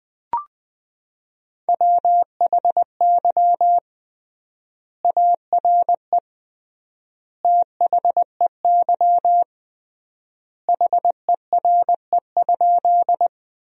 Morse code, heard as a keyed tone, about 21 wpm, 700 Hz. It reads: WHY ARE THEY HERE?
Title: 3x Word-Spacing - Sentences from Top 200 Words - 20wpm